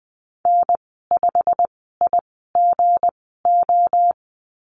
Morse code, heard as a keyed tone, about 19 wpm, spaced as for 14 wpm, 700 Hz. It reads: N5IGO
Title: 179 Lesson 29g - Slash - Call Signs 20wpm